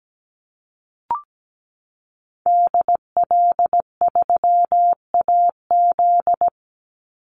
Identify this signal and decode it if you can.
Morse code, heard as a keyed tone, about 17 wpm, 700 Hz. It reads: DL3AZ
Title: Call Signs - 17wpm